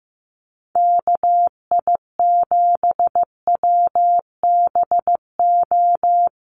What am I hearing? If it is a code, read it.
Morse code, heard as a keyed tone, about 15 wpm, 700 Hz. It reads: KI7WBO
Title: USA Call Signs 15wpm